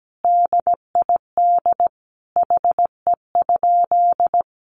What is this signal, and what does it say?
Morse code, heard as a keyed tone, about 17 wpm, 700 Hz. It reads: DID HE?
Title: Sentences from Top 200 Words 17wpm